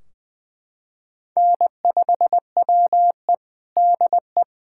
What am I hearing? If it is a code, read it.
Morse code, heard as a keyed tone, about 20 wpm, 700 Hz. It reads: N5WE DE
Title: K1USN SST QSO Practice 20wpm